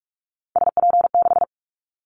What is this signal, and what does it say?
Morse code, heard as a keyed tone, about 44 wpm, 700 Hz. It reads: SP6
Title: Callsign Prefixes for Top 25 DXCC - Repeated 3x - 45wpm